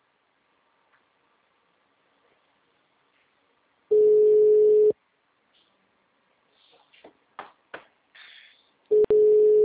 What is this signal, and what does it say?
Ringback tone (Europe, 425 Hz).
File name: out- -2005- - - .43257